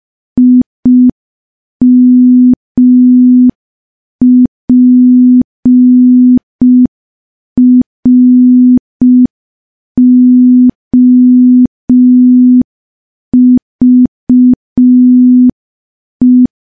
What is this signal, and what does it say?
Morse code, heard as a keyed tone, about 5 wpm, 250 Hz. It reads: IMPROVE
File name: morse / morse-training-05WPM